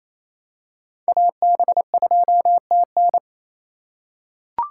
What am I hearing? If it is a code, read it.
Morse code, heard as a keyed tone, about 28 wpm, 700 Hz. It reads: AB2TN
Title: USA Call Signs - 5 Characters - 2x Character Spacing - 28wpm